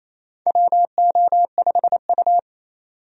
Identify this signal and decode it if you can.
Morse code, heard as a keyed tone, about 28 wpm, 700 Hz. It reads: WO5U
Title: USA Call Signs - 4 Characters - 28wpm